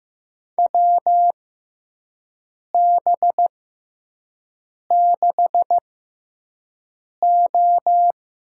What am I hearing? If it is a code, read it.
Morse code, heard as a keyed tone, about 15 wpm, spaced as for 5 wpm, 700 Hz. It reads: WB6O
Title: USA Call Signs - 6 Characters - 6x Character Spacing - 15wpm